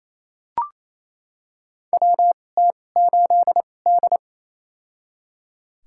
Morse code, heard as a keyed tone, about 28 wpm, spaced as for 20 wpm, 700 Hz. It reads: WT8D
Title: USA Call Signs - 4 Characters - 2x Character Spacing - 28wpm